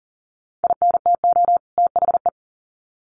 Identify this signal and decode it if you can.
Morse code, heard as a keyed tone, about 40 wpm, 700 Hz. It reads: INTO THE
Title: Sets of 2 Words 40wpm